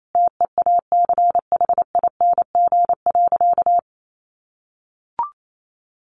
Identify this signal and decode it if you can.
Morse code, heard as a keyed tone, about 28 wpm, 700 Hz. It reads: TEACHING.
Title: Sentences from Top 1100 Words 28wpm